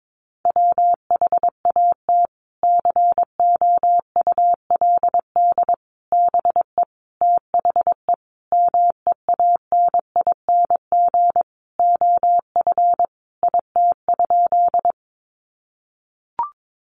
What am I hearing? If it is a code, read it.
Morse code, heard as a keyed tone, about 22 wpm, 700 Hz. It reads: WHAT COULD BE THE MEANING OF IT?
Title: Sentences from Top 900 Words 22wpm